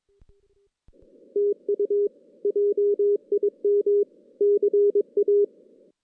Morse code, heard as a keyed tone, about 22 wpm, 410 Hz. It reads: TU JIM CA
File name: 2016 JAN CW NAQP / 2016 Jan naqp k5ot - 0-6